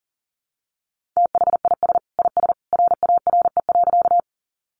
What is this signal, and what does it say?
Morse code, heard as a keyed tone, about 40 wpm, 700 Hz. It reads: THIS IS RARE.
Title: Sentences from Top 1100 Words 40wpm